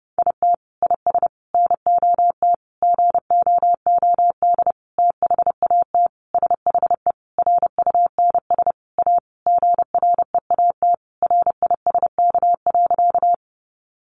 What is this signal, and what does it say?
Morse code, heard as a keyed tone, about 30 wpm, 700 Hz. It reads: IT IS NOT GOOD THAT SHE RUNS A GREAT RISK.
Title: Sentences from Top 600 Words 30wpm